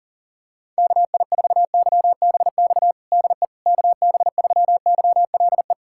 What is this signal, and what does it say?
Morse code, heard as a keyed tone, about 37 wpm, 700 Hz. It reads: KI4YBX DE KB3YLE
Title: USA Callsign DE Callsign - 6 Characters - 40wpm